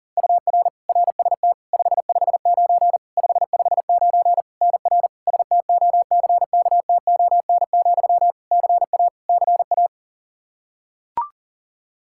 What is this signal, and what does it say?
Morse code, heard as a keyed tone, about 40 wpm, 700 Hz. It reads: UR RST 559 559 NR STOCKTON, CA CA
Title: Practice QSO - Exchange 1 - Beginning - 40wpm